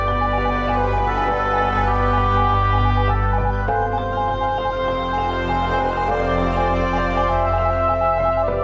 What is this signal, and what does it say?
Morse code, heard as a keyed tone, about 20 wpm, 820 Hz. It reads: SOMETHING TO TELL